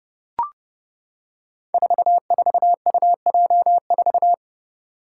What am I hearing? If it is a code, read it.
Morse code, heard as a keyed tone, about 30 wpm, 700 Hz. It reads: 44UJ4
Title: Five Challenging Letter-Number Combinations 30wpm